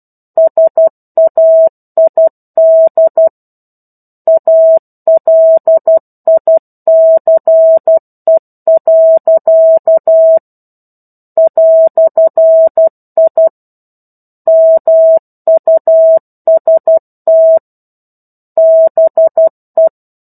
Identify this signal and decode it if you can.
Morse code, heard as a keyed tone, about 12 wpm, 650 Hz. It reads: SAID ALICE. "I MUST BE
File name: 01-Chapter-01 12 WPM